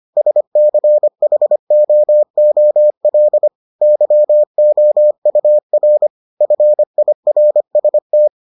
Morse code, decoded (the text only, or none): SCHOOL YOUR FIRST